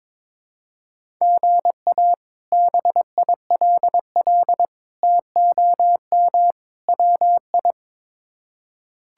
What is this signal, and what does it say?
Morse code, heard as a keyed tone, about 21 wpm, 700 Hz. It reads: GA BILL TOM WI
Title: K1USN SST QSO Practice - 22wpm